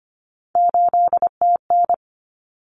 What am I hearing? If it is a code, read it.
Morse code, heard as a keyed tone, about 25 wpm, 700 Hz. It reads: 8TN
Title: Common Cut Numbers - 3-Digit 25wpm